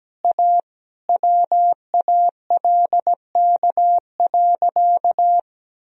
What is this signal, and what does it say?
Morse code, heard as a keyed tone, about 17 wpm, 700 Hz. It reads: A WALK.
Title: Sentences from Top 1000 Words - Review Part 1 - 17wpm